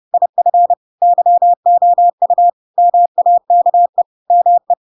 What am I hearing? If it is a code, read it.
Morse code, heard as a keyed tone, about 30 wpm, 700 Hz. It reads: IF YOU MAKE ME
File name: Pinocchio / Pinocchio ch4 30 30 ik1hge